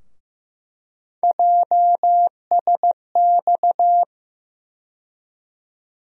Morse code, heard as a keyed tone, about 15 wpm, 700 Hz. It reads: JSX
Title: USA Call Sign Postfixes - 3 Letters 15wpm